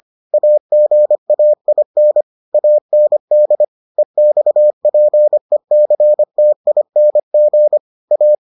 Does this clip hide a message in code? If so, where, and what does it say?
Yes: Morse code from 0.3 to 8.3 s, heard as a keyed tone, about 25 wpm, 600 Hz. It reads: AGAIN AND EXPECTING A